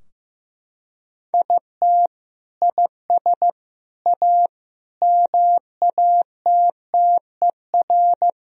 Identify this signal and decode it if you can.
Morse code, heard as a keyed tone, about 15 wpm, 700 Hz. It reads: IT IS A MATTER
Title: Sentences from Top 1000 Words - Review Part 3 - 15wpm